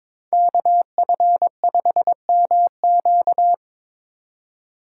Morse code, heard as a keyed tone, about 24 wpm, 700 Hz. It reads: KF5MQ
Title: Speed Racing - USA 2x2 Call Signs 22wpm